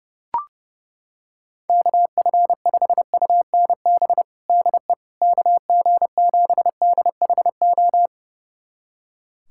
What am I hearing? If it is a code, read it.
Morse code, heard as a keyed tone, about 32 wpm, 700 Hz. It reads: KF5UNB DE KG7DHO
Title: USA Callsign DE Callsign - 6 Characters - 30wpm